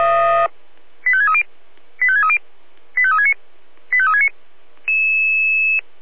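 Five-tone selective call (ZVEI), heard as quick runs of five tones, tones of about 70 ms. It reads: 85630 85630 85369 85369
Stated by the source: ZVEI five-tone selective call received by SDR.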